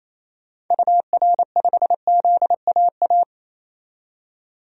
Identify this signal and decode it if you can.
Morse code, heard as a keyed tone, about 28 wpm, 700 Hz. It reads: UR5ZAA
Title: DXCC-100 Call Signs - 6 Characters - 28wpm